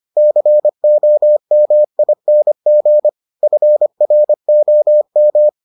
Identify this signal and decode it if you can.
Morse code, heard as a keyed tone, about 25 wpm, 600 Hz. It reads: COMING FROM